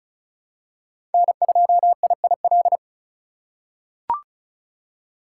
Morse code, heard as a keyed tone, about 34 wpm, 700 Hz. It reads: N2IIL
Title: USA Call Signs - 3x Character Spacing - 35wpm